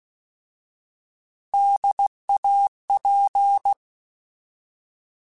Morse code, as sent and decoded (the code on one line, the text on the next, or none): -.. .- .--.
DAP